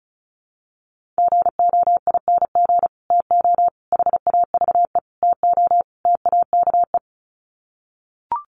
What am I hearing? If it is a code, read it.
Morse code, heard as a keyed tone, about 34 wpm, 700 Hz. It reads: GOING TO HAVE TO TAKE
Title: Sets of 5 Words Encore 35wpm